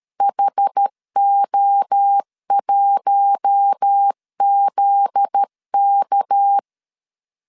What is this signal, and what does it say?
Morse code, heard as a keyed tone, about 13 wpm, 800 Hz. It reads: HO1ZK
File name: CW / DET / 1 9 1500C S C 12wpm 3811